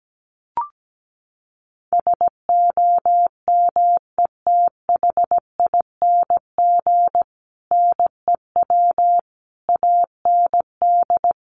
Morse code, heard as a keyed tone, about 17 wpm, 700 Hz. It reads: SOMETHING NEW AND
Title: Sentences from Top 700 Words 17wpm